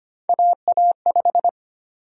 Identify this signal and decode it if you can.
Morse code, heard as a keyed tone, about 25 wpm, 700 Hz. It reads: AA5
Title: USA Call Sign Prefixes - Repeated 2x - 25wpm